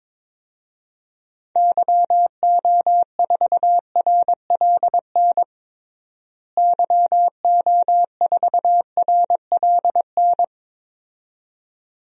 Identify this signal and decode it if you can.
Morse code, heard as a keyed tone, about 22 wpm, 700 Hz. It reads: YO4RLN YO4RLN
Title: DXCC-100 Call Signs - 6 Characters - Repeated 2x - 22wpm